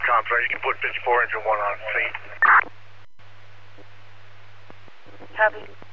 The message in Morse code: none

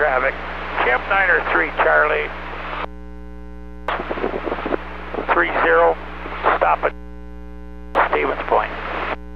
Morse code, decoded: none

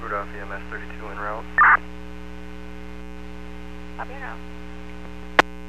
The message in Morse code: none

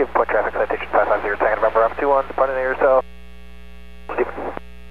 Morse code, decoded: none

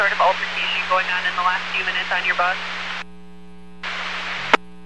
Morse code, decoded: none

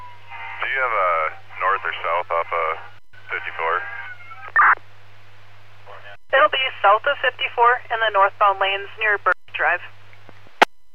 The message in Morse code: none

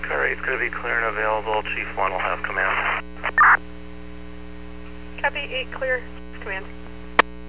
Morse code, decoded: none

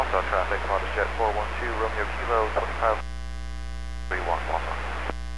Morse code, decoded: none

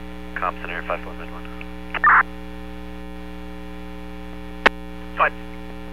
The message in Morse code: none

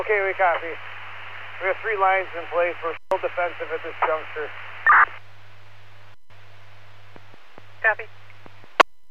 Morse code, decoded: none

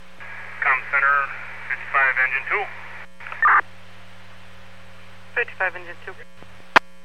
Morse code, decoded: none

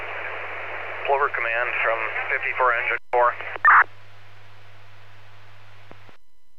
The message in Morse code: none

